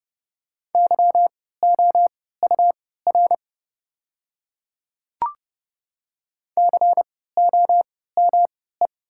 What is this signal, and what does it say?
Morse code, heard as a keyed tone, about 29 wpm, spaced as for 17 wpm, 700 Hz. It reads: YOUR COME
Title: 3x Character Spacing - Top 100 Words - 30wpm